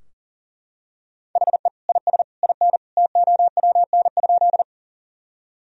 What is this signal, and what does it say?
Morse code, heard as a keyed tone, about 40 wpm, 700 Hz. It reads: HE IS IN TOWN?
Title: Sentences from Top 1000 Words - Review Part 2 - 40wpm